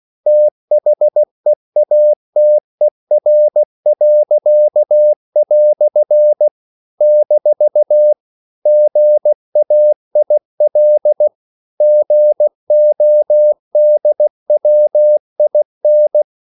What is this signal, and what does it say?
Morse code, heard as a keyed tone, about 16 wpm, 600 Hz. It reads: THEATER." - GAIL GODWIN